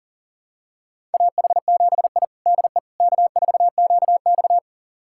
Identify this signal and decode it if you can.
Morse code, heard as a keyed tone, about 40 wpm, 700 Hz. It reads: AH7I DE K4QX
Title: USA Callsign DE Callsign - 4 Characters - 40wpm